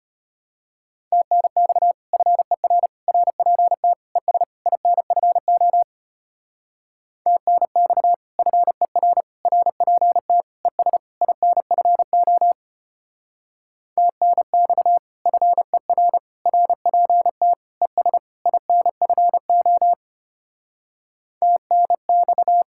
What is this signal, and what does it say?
Morse code, heard as a keyed tone, about 32 wpm, 700 Hz. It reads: TNX FER RPT ES INFO TNX FER RPT ES INFO TNX FER RPT ES INFO TNX